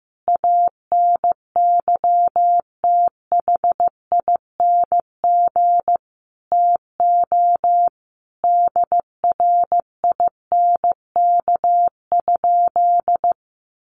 Morse code, heard as a keyed tone, about 15 wpm, 700 Hz. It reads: ANYTHING TO DRINK?
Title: Sentences from Top 1500 Words 15wpm